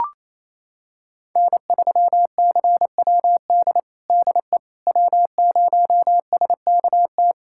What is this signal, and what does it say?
Morse code, heard as a keyed tone, about 28 wpm, 700 Hz. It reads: N3CWD DE W0SKT
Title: USA Callsign DE Callsign - 5 Characters - 28wpm